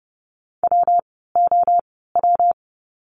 Morse code, heard as a keyed tone, about 30 wpm, spaced as for 17 wpm, 700 Hz. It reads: WOW
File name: course / 30:10wpm / 115 Lesson 19f - Letter w - Words 30wpm